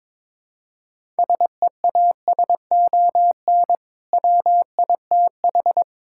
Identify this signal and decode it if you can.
Morse code, heard as a keyed tone, about 22 wpm, 700 Hz. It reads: SEASON WITH
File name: practice / 22wpm / Build A Phrase - 5-Word Phrases - 22wpm